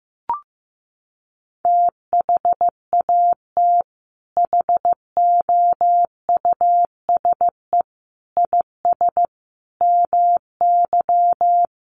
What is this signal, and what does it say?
Morse code, heard as a keyed tone, about 15 wpm, 700 Hz. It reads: THAT HOUSE IS MY
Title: Sentences from Top 1000 Words 15wpm